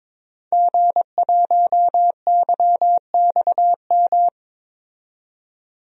Morse code, heard as a keyed tone, about 21 wpm, 700 Hz. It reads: G1YXM
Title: DXCC-100 Call Signs - 5 Characters - 22wpm